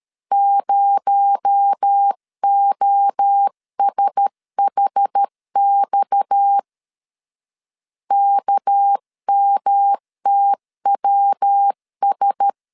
Morse code, heard as a keyed tone, about 13 wpm, 800 Hz. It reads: 0OSHX KMTWS
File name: CW / DET / 1 6 1500C S C 12wpm 3533